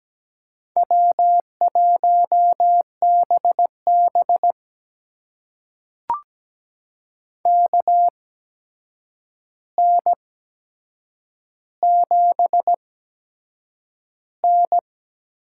Morse code, heard as a keyed tone, about 17 wpm, spaced as for 7 wpm, 700 Hz. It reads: W1BB KN7N
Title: USA Call Signs - 4 Characters - 8x Character Spacing - 17wpm